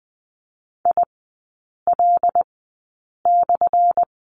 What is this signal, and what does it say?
Morse code, heard as a keyed tone, about 20 wpm, spaced as for 8 wpm, 700 Hz. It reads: IL/
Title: 184 Lesson 30e - Letter b - Three Characters 20wpm